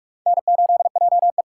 Morse code, heard as a keyed tone, about 44 wpm, 700 Hz. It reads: N8JE